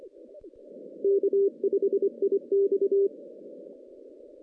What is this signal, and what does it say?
Morse code, heard as a keyed tone, about 25 wpm, 400 Hz. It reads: K5IX
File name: 2016 JAN CW NAQP / 2016 Jan naqp k5ot - 0-10